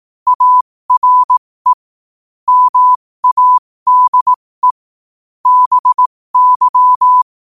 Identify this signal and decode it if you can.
Morse code, heard as a keyed tone, about 18 wpm, spaced as for 15 wpm, 1000 Hz. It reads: ARE MADE BY
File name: qotdfiles / 15wpm